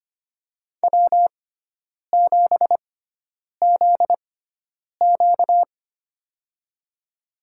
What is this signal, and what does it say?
Morse code, heard as a keyed tone, about 25 wpm, spaced as for 9 wpm, 700 Hz. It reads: W7ZQ
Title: USA Call Signs - 6x Character Spacing - 25wpm